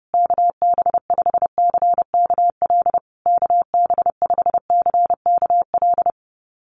Morse code, heard as a keyed tone, about 30 wpm, 700 Hz. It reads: KB5CKL KB5CKL
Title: USA Call Signs Repeated 3x 30wpm